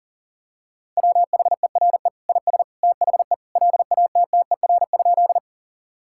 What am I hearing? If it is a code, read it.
Morse code, heard as a keyed tone, about 40 wpm, 700 Hz. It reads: WHERE IS THE LATTER?
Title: Sentences from Top 1600 Words 40wpm